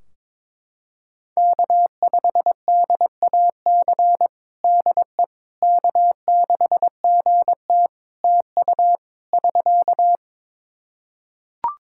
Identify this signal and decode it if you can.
Morse code, heard as a keyed tone, about 22 wpm, 700 Hz. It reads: K5DAC DE K6GT TU <SK>
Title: Practice QSO - Exchange 4 - Ending - 22wpm